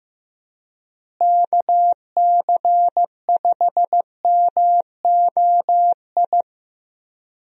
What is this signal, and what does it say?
Morse code, heard as a keyed tone, about 15 wpm, 700 Hz. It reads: KC5MOI
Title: USA Call Signs - 6 Characters - Repeated 3x - 15wpm